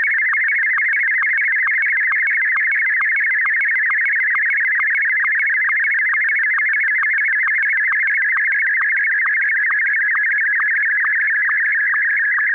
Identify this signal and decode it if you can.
SSTV (slow-scan television), Martin 1.